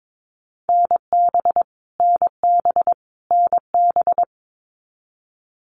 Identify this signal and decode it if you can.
Morse code, heard as a keyed tone, about 22 wpm, 700 Hz. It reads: NB NB NB
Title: Canadian Provinces and Territories - Repeated 3x - 22wpm